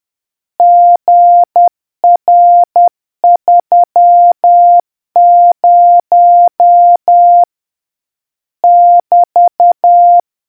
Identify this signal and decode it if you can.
Morse code, heard as a keyed tone, about 10 wpm, 700 Hz. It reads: GR30 <BT>